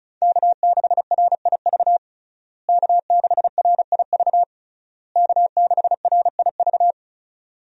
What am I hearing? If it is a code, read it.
Morse code, heard as a keyed tone, about 36 wpm, 700 Hz. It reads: K6RIV K6RIV K6RIV
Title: USA Call Signs - 5 Characters - Repeated 3x - 35wpm